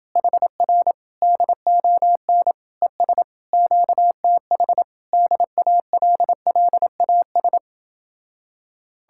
Morse code, heard as a keyed tone, about 29 wpm, 700 Hz. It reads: HR DON ES QTH DALLAS